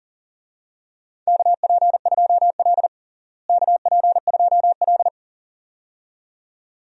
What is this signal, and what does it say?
Morse code, heard as a keyed tone, about 40 wpm, 700 Hz. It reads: KP2L KP2L
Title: DXCC-100 Call Signs - 4 Characters - Repeated 2x - 40wpm